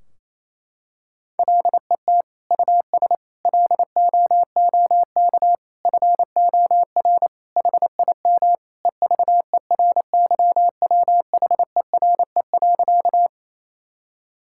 Morse code, heard as a keyed tone, about 28 wpm, 700 Hz. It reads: LET US LOOK FOR HIM EVERYWHERE.
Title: Sentences from Top 1600 Words 28wpm